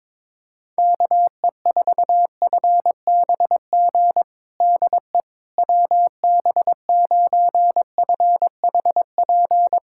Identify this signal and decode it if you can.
Morse code, heard as a keyed tone, about 22 wpm, 700 Hz. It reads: KE4FBG DE WB9FHP
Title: USA Callsign DE Callsign - 6 Characters - 22wpm